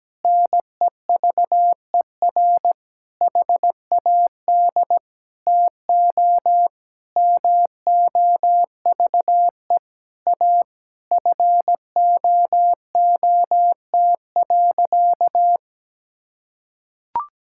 Morse code, heard as a keyed tone, about 17 wpm, 700 Hz. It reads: NEVER HAD TO MOVE A FOOT.
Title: Sentences from Top 1600 Words 17wpm